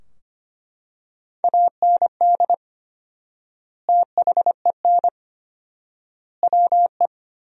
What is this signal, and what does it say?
Morse code, heard as a keyed tone, about 25 wpm, 700 Hz. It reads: AND THEN WE
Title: 4x Word-Spacing - Sentences from Top 300 Words - 25wpm